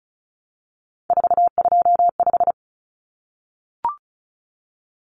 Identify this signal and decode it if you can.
Morse code, heard as a keyed tone, about 34 wpm, 700 Hz. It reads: 425
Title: Numbers - 3-Digit 35wpm